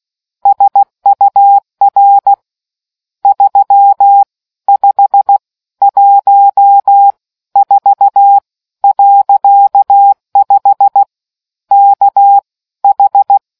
Morse code, heard as a keyed tone, about 16 wpm, spaced as for 13 wpm, 800 Hz. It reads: SUR 3514.5 KH